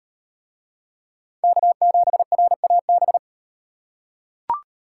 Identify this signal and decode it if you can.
Morse code, heard as a keyed tone, about 34 wpm, 700 Hz. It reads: K7RAB